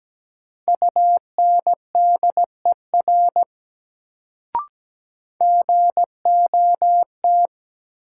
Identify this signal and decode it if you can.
Morse code, heard as a keyed tone, about 17 wpm, 700 Hz. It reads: UNDER GOT
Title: Sets of 2 Words - New Word Drill - 17wpm